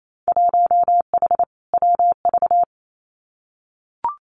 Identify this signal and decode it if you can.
Morse code, heard as a keyed tone, about 28 wpm, 700 Hz. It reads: 1H WV
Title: Winter Field Day Exchanges Encore 28wpm